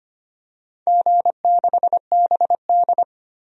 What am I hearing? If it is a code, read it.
Morse code, heard as a keyed tone, about 25 wpm, 700 Hz. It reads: G6BD
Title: DXCC-100 Call Signs - 4 Characters - 25wpm